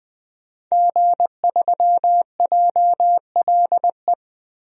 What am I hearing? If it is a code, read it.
Morse code, heard as a keyed tone, about 20 wpm, 700 Hz. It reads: G3JLE
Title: DXCC-100 Call Signs - 5 Characters - 20wpm